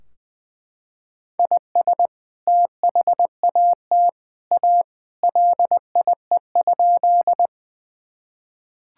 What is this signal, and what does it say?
Morse code, heard as a keyed tone, about 20 wpm, 700 Hz. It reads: IS THAT A LIE?
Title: Sentences from Top 1600 Words 20wpm